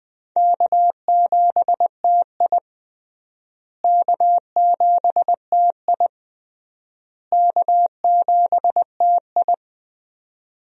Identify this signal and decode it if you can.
Morse code, heard as a keyed tone, about 20 wpm, 700 Hz. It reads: K7TI K7TI K7TI
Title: USA Call Signs - 4 Characters - Repeated 3x - 20wpm